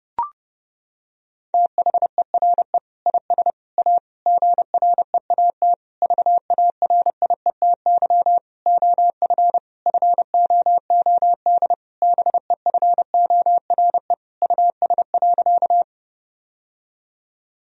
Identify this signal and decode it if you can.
Morse code, heard as a keyed tone, about 32 wpm, 700 Hz. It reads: THERE IS A GREAT VARIETY OF FOOD BEFORE US.
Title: Sentences from Top 1000 Words - Review Part 4 - 30wpm